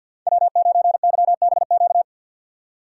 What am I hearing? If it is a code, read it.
Morse code, heard as a keyed tone, about 53 wpm, 700 Hz. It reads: W9YBQ
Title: USA Call Signs - 5 Characters - 6x Character Spacing - 50wpm